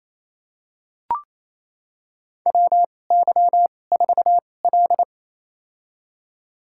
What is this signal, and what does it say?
Morse code, heard as a keyed tone, about 27 wpm, spaced as for 20 wpm, 700 Hz. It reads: WY4L
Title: USA Call Signs - 2x Character Spacing - 28wpm